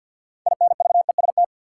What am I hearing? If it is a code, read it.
Morse code, heard as a keyed tone, about 53 wpm, 700 Hz. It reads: INVEST
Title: Sentences from Top 1100 Words - New Word Drill Alt - 50wpm